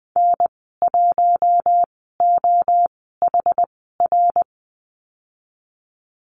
Morse code, heard as a keyed tone, about 19 wpm, spaced as for 14 wpm, 700 Hz. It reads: N1OHR
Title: 161 Lesson 26g - Letter g - Call Signs 20wpm